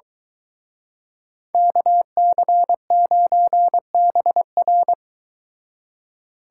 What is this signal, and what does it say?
Morse code, heard as a keyed tone, about 23 wpm, 700 Hz. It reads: KC9BR